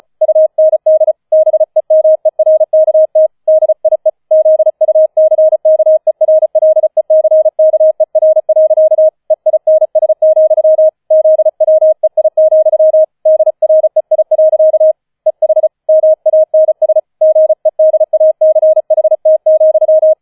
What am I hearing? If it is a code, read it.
Morse code, heard as a keyed tone, about 34 wpm, 630 Hz. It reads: UND BEMERKT DIE ZUCKERLECKER. EINS, ZWEI, DREI. EH MANS GEDACHT,